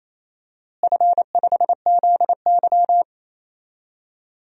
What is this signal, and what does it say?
Morse code, heard as a keyed tone, about 28 wpm, 700 Hz. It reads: F5ZY
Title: DXCC-100 Call Signs - 4 Characters - Repeated 2x - 28wpm